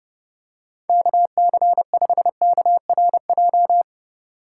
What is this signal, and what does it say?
Morse code, heard as a keyed tone, about 32 wpm, 700 Hz. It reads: KC5KRJ
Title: USA Call Signs - 6 Characters - 3x Character Spacing - 30wpm